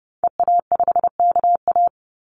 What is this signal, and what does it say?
Morse code, heard as a keyed tone, about 30 wpm, 700 Hz. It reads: EA5KA
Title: Call Signs Repeated 2x - 30wpm